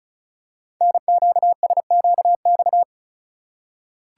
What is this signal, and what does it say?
Morse code, heard as a keyed tone, about 36 wpm, 700 Hz. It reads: NQSQX
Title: Code Groups - 5 Letters - 35wpm